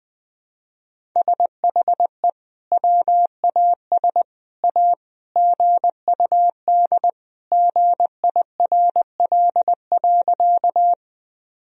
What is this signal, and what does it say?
Morse code, heard as a keyed tone, about 21 wpm, 700 Hz. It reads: SHE WAS A GUD GIRL.
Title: Sentences from Top 400 Words and Abbreviations 20wpm